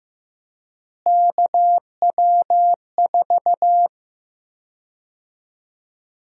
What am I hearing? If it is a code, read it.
Morse code, heard as a keyed tone, about 16 wpm, 700 Hz. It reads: KW4
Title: USA Call Sign Prefixes - Repeated 2x - 15wpm